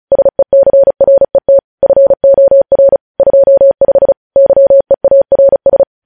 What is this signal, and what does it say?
Morse code, heard as a keyed tone, about 36 wpm, 550 Hz. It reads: SECRET FOR 25 YEARS
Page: news headlines in morse code at 35 wpm